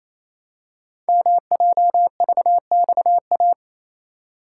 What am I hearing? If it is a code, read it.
Morse code, heard as a keyed tone, about 27 wpm, 700 Hz. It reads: MJVXA
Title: Code Groups - 5 Letters - 28wpm